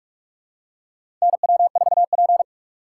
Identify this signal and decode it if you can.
Morse code, heard as a keyed tone, about 45 wpm, 700 Hz. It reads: NW4P
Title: USA Call Signs - 4 Characters - 4x Character Spacing - 45wpm